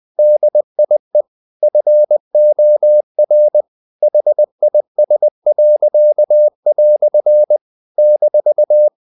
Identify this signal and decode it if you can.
Morse code, heard as a keyed tone, about 20 wpm, 600 Hz. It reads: DIE FOR HIS." -
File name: CW / Practice / EnglishQuotations0320